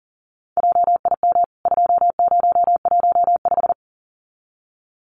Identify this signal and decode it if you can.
Morse code, heard as a keyed tone, about 40 wpm, 700 Hz. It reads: JIM 2015
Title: CWT Exchange 40wpm